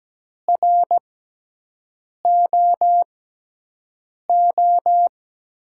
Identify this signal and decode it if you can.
Morse code, heard as a keyed tone, about 17 wpm, spaced as for 6 wpm, 700 Hz. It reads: ROO
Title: USA Call Signs - 6 Characters - 6x Character Spacing - 17wpm